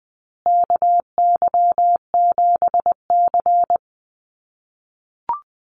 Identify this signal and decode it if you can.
Morse code, heard as a keyed tone, about 20 wpm, 700 Hz. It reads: KY7C
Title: USA Call Signs Repeated 3x 20wpm